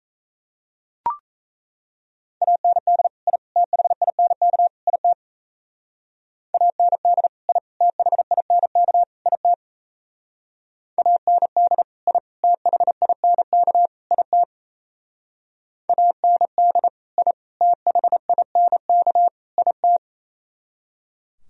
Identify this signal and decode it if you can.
Morse code, heard as a keyed tone, about 36 wpm, 700 Hz. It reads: AND I THINK IT AND I THINK IT AND I THINK IT AND I THINK IT